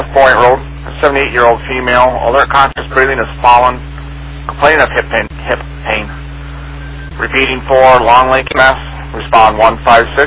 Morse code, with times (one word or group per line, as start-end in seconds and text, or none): none